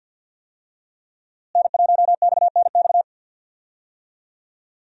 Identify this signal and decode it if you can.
Morse code, heard as a keyed tone, about 53 wpm, 700 Hz. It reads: N1XNX
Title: USA Call Signs - 5 Characters - Repeated 3x - 50wpm